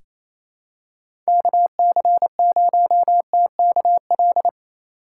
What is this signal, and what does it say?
Morse code, heard as a keyed tone, about 28 wpm, 700 Hz. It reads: KC0TKL
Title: USA Call Signs - 6 Characters - 4x Character Spacing - 28wpm